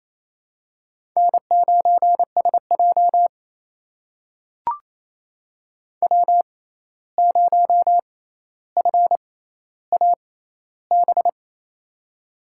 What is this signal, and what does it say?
Morse code, heard as a keyed tone, about 28 wpm, spaced as for 13 wpm, 700 Hz. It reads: N9SJ W0FAB
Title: USA Call Signs - 6x Character Spacing - 28wpm